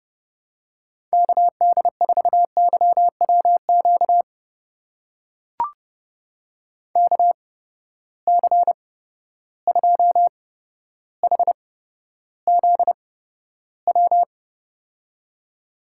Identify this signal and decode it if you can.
Morse code, heard as a keyed tone, about 30 wpm, spaced as for 13 wpm, 700 Hz. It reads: KD4YWQ KC2HZW
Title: USA Call Signs - 6 Characters - 8x Character Spacing - 30wpm